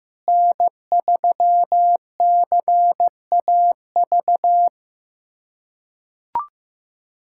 Morse code, heard as a keyed tone, about 15 wpm, 700 Hz. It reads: N3CAV
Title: USA Callsign DE Callsign 15wpm